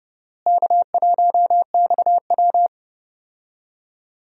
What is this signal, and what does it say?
Morse code, heard as a keyed tone, about 31 wpm, 700 Hz. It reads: K1XW
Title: USA Call Signs - 4 Characters - Repeated 2x - 30wpm